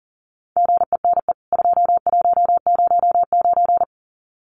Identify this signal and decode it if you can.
Morse code, heard as a keyed tone, about 40 wpm, 700 Hz. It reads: GENE 2109
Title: CWT Exchange 40wpm